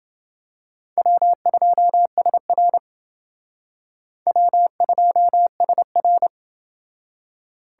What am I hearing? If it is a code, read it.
Morse code, heard as a keyed tone, about 29 wpm, 700 Hz. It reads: W2SR W2SR